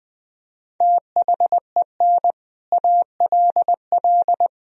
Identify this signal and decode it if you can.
Morse code, heard as a keyed tone, about 21 wpm, 700 Hz. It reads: THEN ALL
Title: Sentences from Top 500 Words and Abbreviations 20wpm